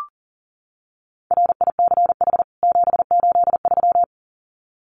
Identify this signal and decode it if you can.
Morse code, heard as a keyed tone, about 40 wpm, 700 Hz. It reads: RICH 783
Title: CWT Exchange 40wpm